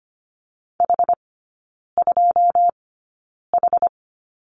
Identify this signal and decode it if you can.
Morse code, heard as a keyed tone, about 25 wpm, spaced as for 9 wpm, 690 Hz. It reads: H2H
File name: Lesson 14e - Number 2 - Three Characters 25wpm